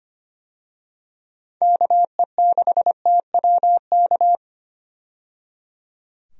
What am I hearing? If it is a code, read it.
Morse code, heard as a keyed tone, about 25 wpm, 700 Hz. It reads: KE6TWK
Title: USA Call Signs - 6 Characters - 25wpm